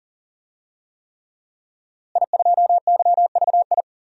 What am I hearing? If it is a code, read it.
Morse code, heard as a keyed tone, about 40 wpm, 700 Hz. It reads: I2YVI
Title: DXCC-100 Call Signs - 5 Characters - Repeated 3x - 40wpm